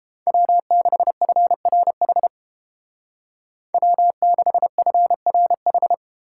Morse code, heard as a keyed tone, about 28 wpm, 700 Hz. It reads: W6FRH W6FRH